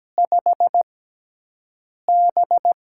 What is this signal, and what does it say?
Morse code, heard as a keyed tone, about 17 wpm, spaced as for 6 wpm, 700 Hz. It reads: 5B
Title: USA Call Signs - 6 Characters - 6x Character Spacing - 17wpm